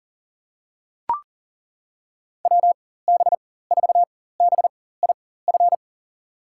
Morse code, heard as a keyed tone, about 40 wpm, spaced as for 19 wpm, 700 Hz. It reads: WB4BIF
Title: USA Call Signs - 6 Characters - 4x Character Spacing - 40wpm